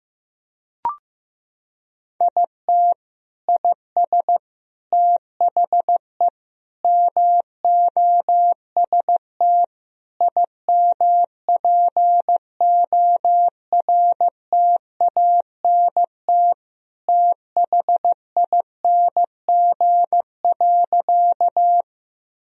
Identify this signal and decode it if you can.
Morse code, heard as a keyed tone, about 15 wpm, 700 Hz. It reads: IT IS THE MOST IMPORTANT THING.
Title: Sentences from Top 1000 Words - Review Part 1 - 15wpm